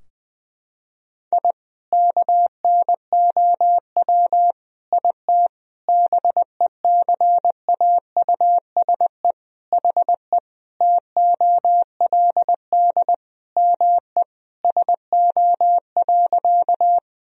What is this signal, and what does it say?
Morse code, heard as a keyed tone, about 20 wpm, 700 Hz. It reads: I KNOW IT BECAUSE HE TOLD ME SO.
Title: Sentences from Top 1000 Words - Review Part 3 - 20wpm